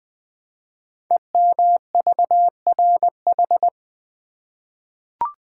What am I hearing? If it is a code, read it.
Morse code, heard as a keyed tone, about 20 wpm, 700 Hz. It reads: EMVRH
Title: Code Groups - 5 Letters - 20wpm